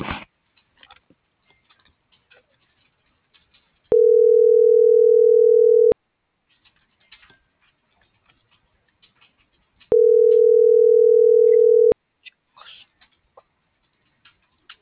Ringback tone (North America), 440 + 480 Hz.